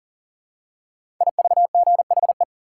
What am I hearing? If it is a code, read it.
Morse code, heard as a keyed tone, about 40 wpm, 700 Hz. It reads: IVGHE
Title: Code Groups - 5 Letters - 40wpm